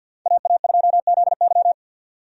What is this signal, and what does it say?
Morse code, heard as a keyed tone, about 47 wpm, 700 Hz. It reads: AA2ZY
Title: USA Call Signs - 5 Characters - 50wpm